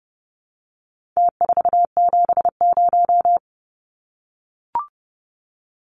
Morse code, heard as a keyed tone, about 30 wpm, 700 Hz. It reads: T470
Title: Numbers - Contest Style - 4-Digit 30wpm